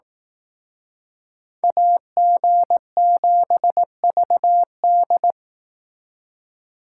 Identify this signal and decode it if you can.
Morse code, heard as a keyed tone, about 18 wpm, 700 Hz. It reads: AG7VD
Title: Speed Racing - USA Call Signs 15wpm